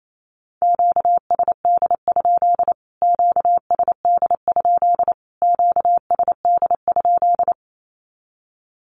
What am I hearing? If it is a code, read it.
Morse code, heard as a keyed tone, about 28 wpm, 700 Hz. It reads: QSD? QSD? QSD?
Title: Q-Codes Repeated 3x 28wpm